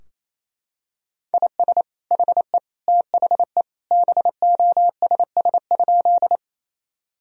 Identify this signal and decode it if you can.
Morse code, heard as a keyed tone, about 28 wpm, 700 Hz. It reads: IS HE THE BOSS?
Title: Sentences from Top 1000 Words - Review Part 4 - 28wpm